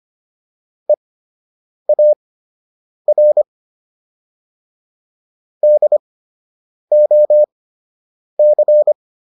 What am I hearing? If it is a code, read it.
Morse code, heard as a keyed tone, about 25 wpm, spaced as for 8 wpm, 600 Hz. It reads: EAR DOC